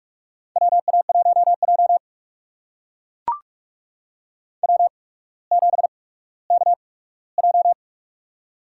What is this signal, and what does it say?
Morse code, heard as a keyed tone, about 44 wpm, spaced as for 19 wpm, 700 Hz. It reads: WA1J W7KJ
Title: USA Call Signs - 4 Characters - 8x Character Spacing - 45wpm